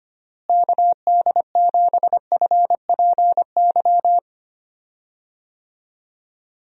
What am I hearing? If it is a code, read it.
Morse code, heard as a keyed tone, about 25 wpm, 700 Hz. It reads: KD7FPY
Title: USA Call Signs - 6 Characters - Repeated 3x - 25wpm